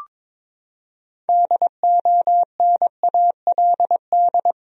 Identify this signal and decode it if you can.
Morse code, heard as a keyed tone, about 22 wpm, 700 Hz. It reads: DONALD
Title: USA Common Names - 22wpm